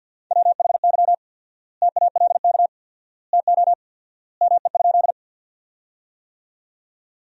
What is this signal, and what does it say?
Morse code, heard as a keyed tone, about 47 wpm, 700 Hz. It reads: WHY TALK TO ME?
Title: 4x Word-Spacing - Sentences from Top 400 Words - 50wpm